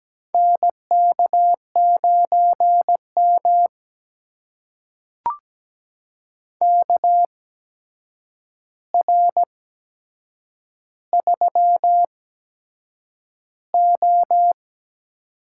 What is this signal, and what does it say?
Morse code, heard as a keyed tone, about 17 wpm, spaced as for 7 wpm, 700 Hz. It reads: NK9M KR3O
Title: USA Call Signs - 4 Characters - 8x Character Spacing - 17wpm